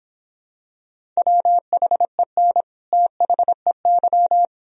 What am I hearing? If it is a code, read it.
Morse code, heard as a keyed tone, about 26 wpm, 700 Hz. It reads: WHEN THEY